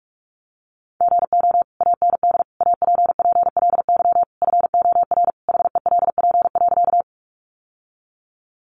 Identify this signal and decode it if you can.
Morse code, heard as a keyed tone, about 44 wpm, 700 Hz. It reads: GO AND APPLY FOR HELP.
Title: Sentences from Top 1200 Words 45wpm